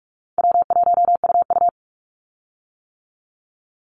Morse code, heard as a keyed tone, about 46 wpm, 700 Hz. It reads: W1UU
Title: Easy Call Signs - 45wpm